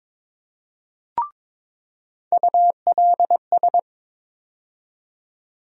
Morse code, heard as a keyed tone, about 21 wpm, 700 Hz. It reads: ULS
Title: USA Call Sign Postfixes - 3 Letters 22wpm